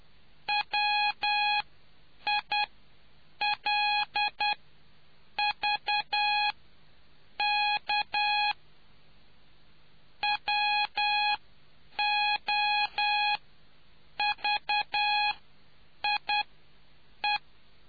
Morse code, heard as a keyed tone, about 10 wpm, spaced as for 7 wpm, 830 Hz. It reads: WILVK WOVIE